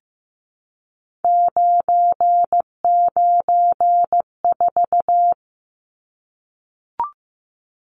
Morse code, heard as a keyed tone, about 15 wpm, 700 Hz. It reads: 994
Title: Numbers - 3-Digit 15wpm